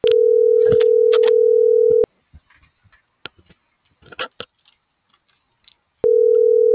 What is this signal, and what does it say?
Ringback tone (North America), 440 + 480 Hz.